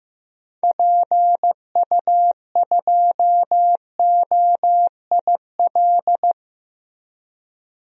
Morse code, heard as a keyed tone, about 15 wpm, 700 Hz. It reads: PU2OIL
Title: DXCC-100 Call Signs - 6 Characters - 15wpm